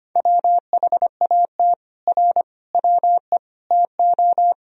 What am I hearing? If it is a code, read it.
Morse code, heard as a keyed tone, about 25 wpm, 700 Hz. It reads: WHAT R WE TO
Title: Sentences from Top 100 Words and Abbreviations 25wpm